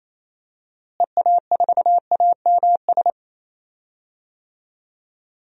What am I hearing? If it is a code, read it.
Morse code, heard as a keyed tone, about 28 wpm, 700 Hz. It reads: EA4AMS
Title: DXCC-100 Call Signs - 6 Characters - Repeated 3x - 28wpm